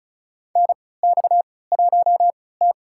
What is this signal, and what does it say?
Morse code, heard as a keyed tone, about 34 wpm, spaced as for 20 wpm, 700 Hz. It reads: NX1T
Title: USA Call Signs - 4 Characters - 3x Character Spacing - 35wpm